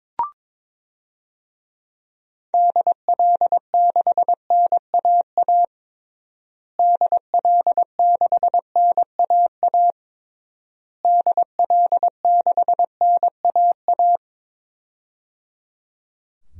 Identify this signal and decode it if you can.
Morse code, heard as a keyed tone, about 22 wpm, 700 Hz. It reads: DL6NAA DL6NAA DL6NAA
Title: DXCC-100 Call Signs - 6 Characters - Repeated 3x - 22wpm